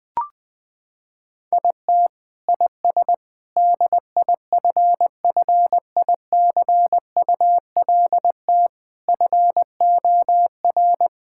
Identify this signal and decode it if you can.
Morse code, heard as a keyed tone, about 21 wpm, 700 Hz. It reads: IT IS DIFFICULT FOR
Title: Sentences from Top 1000 Words - Review Part 6 - 20wpm